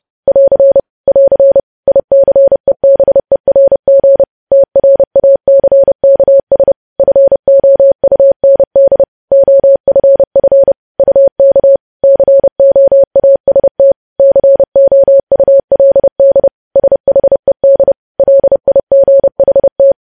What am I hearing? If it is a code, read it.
Morse code, heard as a keyed tone, about 30 wpm, 550 Hz. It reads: <AR> <AR> ICEBERG TRACKS FOUND OFF UK COAST COULD SHED LIGHT
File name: podcast / News headlines-1x-30x30